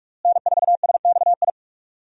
Morse code, heard as a keyed tone, about 46 wpm, 700 Hz. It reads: N4SXI
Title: USA Call Signs - 5 Characters - Repeated 3x - 45wpm